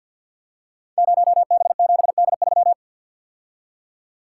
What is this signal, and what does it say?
Morse code, heard as a keyed tone, about 48 wpm, 700 Hz. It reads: 0B7D3
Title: Code Groups - 5 Letter:Numbers - 50wpm